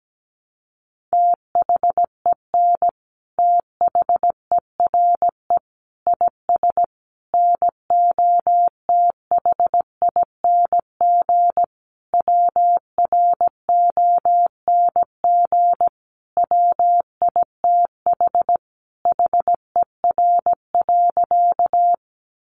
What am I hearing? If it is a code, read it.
Morse code, heard as a keyed tone, about 17 wpm, 700 Hz. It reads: THEN THERE IS NOTHING WRONG WITH HER.
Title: Sentences from Top 900 Words 17wpm